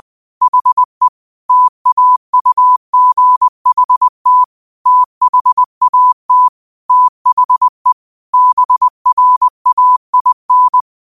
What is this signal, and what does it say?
Morse code, heard as a keyed tone, about 20 wpm, 1000 Hz. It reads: HE TAUGHT THAT THE BRAIN
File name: qotdfiles / 20wpm